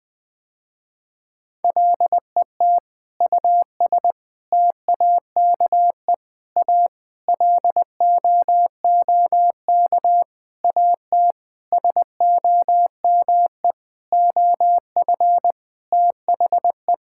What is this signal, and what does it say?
Morse code, heard as a keyed tone, about 20 wpm, 700 Hz. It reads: LET US TAKE A LOOK AT SOME OF THE
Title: Sentences with 8-Word Phrases - Spoken Beginning - 20wpm